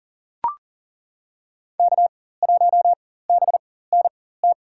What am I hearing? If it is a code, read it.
Morse code, heard as a keyed tone, about 40 wpm, spaced as for 19 wpm, 700 Hz. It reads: K1BNT
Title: USA Call Signs - 5 Characters - 4x Character Spacing - 40wpm